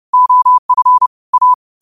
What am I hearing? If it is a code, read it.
Morse code, heard as a keyed tone, about 30 wpm, 1000 Hz. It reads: OF A
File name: qotdfiles / 30wpm